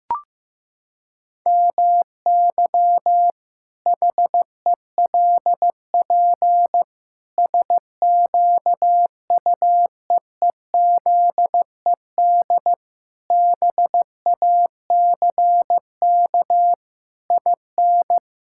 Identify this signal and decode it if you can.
Morse code, heard as a keyed tone, about 15 wpm, 700 Hz. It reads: MY HELP SQUEEZED BACK IN
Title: Pangram Challenge v2 - 15wpm